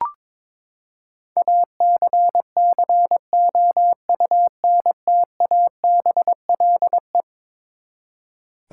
Morse code, heard as a keyed tone, about 22 wpm, 700 Hz. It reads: ACCOUNTABLE
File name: practice / 22wpm / Eleven Letter Words - 22wpm